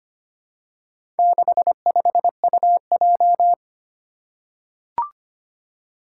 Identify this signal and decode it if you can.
Morse code, heard as a keyed tone, about 25 wpm, 700 Hz. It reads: 65UJ
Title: Four Challenging Letter-Number Combinations 25wpm